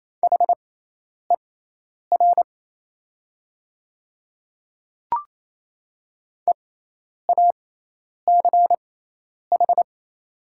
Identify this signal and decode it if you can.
Morse code, heard as a keyed tone, about 27 wpm, spaced as for 10 wpm, 700 Hz. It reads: HER EACH
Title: 6x Character Spacing - Top 100 Words - 28wpm